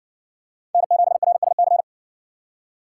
Morse code, heard as a keyed tone, about 57 wpm, 700 Hz. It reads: N7RSC